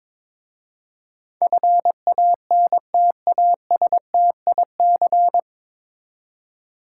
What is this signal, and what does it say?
Morse code, heard as a keyed tone, about 22 wpm, 700 Hz. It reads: FANTASTIC
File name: 9 Letter Words - First 3 Letters Given - 22wpm